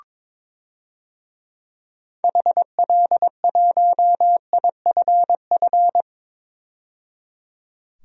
Morse code, heard as a keyed tone, about 22 wpm, 700 Hz. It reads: HL1IFF
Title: DXCC-100 Call Signs - 6 Characters - 22wpm